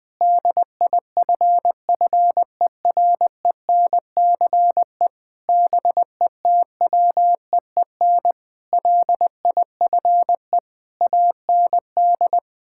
Morse code, heard as a keyed tone, about 21 wpm, 700 Hz. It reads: DIFFERENCE BETWEEN LIFE AND
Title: Sentences with 8-Word Phrases - Spoken Beginning - 20wpm